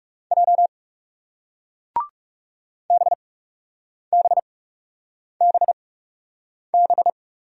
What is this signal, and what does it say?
Morse code, heard as a keyed tone, about 41 wpm, spaced as for 8 wpm, 700 Hz. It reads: J BBBB